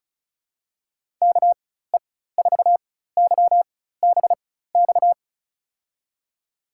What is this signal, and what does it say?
Morse code, heard as a keyed tone, about 36 wpm, spaced as for 16 wpm, 700 Hz. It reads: KE4YBX
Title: USA Call Signs - 6 Characters - 4x Character Spacing - 35wpm